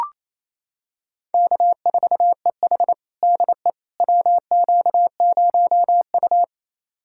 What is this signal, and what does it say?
Morse code, heard as a keyed tone, about 28 wpm, 700 Hz. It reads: K4EH DE WQ0U
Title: USA Callsign DE Callsign - 4 Characters - 28wpm